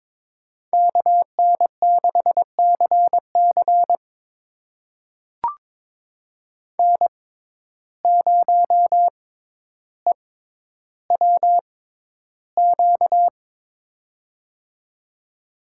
Morse code, heard as a keyed tone, about 21 wpm, spaced as for 11 wpm, 700 Hz. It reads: KN6CC N0EWQ
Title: USA Call Signs - 5 Characters - 6x Character Spacing - 22wpm